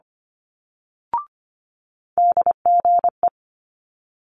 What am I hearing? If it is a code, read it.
Morse code, heard as a keyed tone, about 25 wpm, 700 Hz. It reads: DGE
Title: Multi-letter Phonemes Quick Fire 25wpm